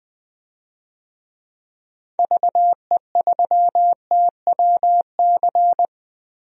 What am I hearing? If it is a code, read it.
Morse code, heard as a keyed tone, about 20 wpm, 700 Hz. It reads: VE3TWC
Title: DXCC-100 Call Signs - 6 Characters - Repeated 3x - 20wpm